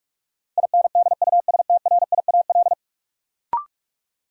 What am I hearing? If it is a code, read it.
Morse code, heard as a keyed tone, about 42 wpm, 700 Hz. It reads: INDUSTRIAL
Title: Sets of 5 Words - New Word Drill - 45wpm